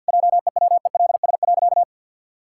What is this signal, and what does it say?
Morse code, heard as a keyed tone, about 51 wpm, 700 Hz. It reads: JEWELS.
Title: Pangram Challenge v2 - 50wpm